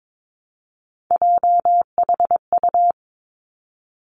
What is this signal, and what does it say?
Morse code, heard as a keyed tone, about 22 wpm, 700 Hz. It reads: JHU
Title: Three Challenging Letter-Number Combinations 22wpm